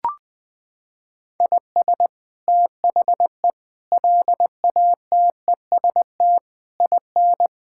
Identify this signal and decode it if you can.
Morse code, heard as a keyed tone, about 21 wpm, 700 Hz. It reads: IS THE LATEST IN
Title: Sets of 8 Words - 20wpm